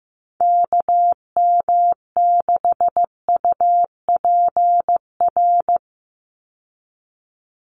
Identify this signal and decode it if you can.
Morse code, heard as a keyed tone, about 15 wpm, 700 Hz. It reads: KM6UPR
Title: USA Call Signs 15wpm